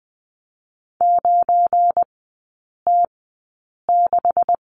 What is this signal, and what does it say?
Morse code, heard as a keyed tone, about 20 wpm, 700 Hz. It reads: 9 T 6
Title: 184 Lesson 30e - Letter b - Three Characters 20wpm